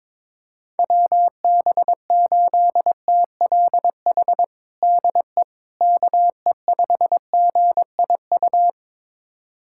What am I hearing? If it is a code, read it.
Morse code, heard as a keyed tone, about 22 wpm, 700 Hz. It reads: WB8TLH DE KE5GIU
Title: USA Callsign DE Callsign - 6 Characters - 22wpm